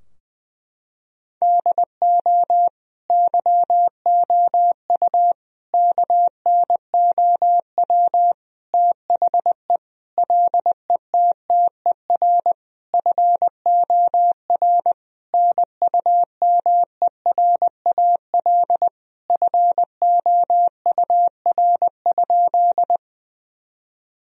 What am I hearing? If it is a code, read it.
Morse code, heard as a keyed tone, about 20 wpm, 700 Hz. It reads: DO YOU KNOW THE LETTER FOR NUMERAL FOUR?
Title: Sentences from Top 1000 Words - Review Part 2 - 20wpm